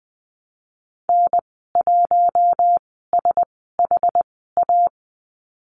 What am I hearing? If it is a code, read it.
Morse code, heard as a keyed tone, about 20 wpm, spaced as for 15 wpm, 700 Hz. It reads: N1SHA
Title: 088 Lesson 15g - Number 5 - Call Signs 20wpm